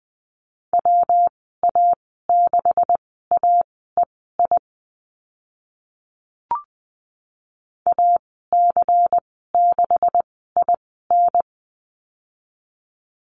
Morse code, heard as a keyed tone, about 20 wpm, spaced as for 14 wpm, 700 Hz. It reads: WA6AEI AC6IN
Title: 140 Lesson 23g - Letter f - Call Signs 20wpm